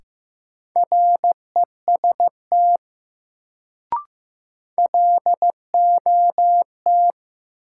Morse code, heard as a keyed tone, about 15 wpm, 700 Hz. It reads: REST LOT
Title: Sentences from Top 500 Words - New Word Drill - 15wpm